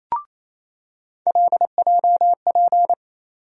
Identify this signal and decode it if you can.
Morse code, heard as a keyed tone, about 28 wpm, 700 Hz. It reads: LJP
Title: USA Call Sign Postfixes - 3 Letters 28wpm